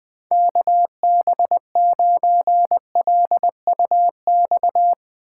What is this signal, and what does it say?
Morse code, heard as a keyed tone, about 20 wpm, 700 Hz. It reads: KB9LUX
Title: Speed Racing - USA 2x3 Call Signs 17wpm